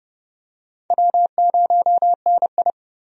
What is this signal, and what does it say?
Morse code, heard as a keyed tone, about 32 wpm, 700 Hz. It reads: W0NI
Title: USA Call Signs - 4 Characters - 30wpm